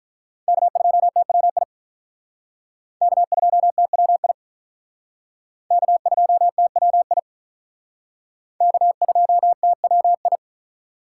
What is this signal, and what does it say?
Morse code, heard as a keyed tone, about 45 wpm, 700 Hz. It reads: K2TWI K2TWI K2TWI K2TWI